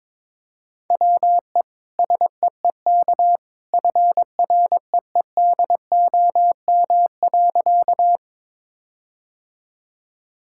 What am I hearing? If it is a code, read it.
Morse code, heard as a keyed tone, about 22 wpm, 700 Hz. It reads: WE SEEK FREEDOM.
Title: Sentences from Top 1600 Words 22wpm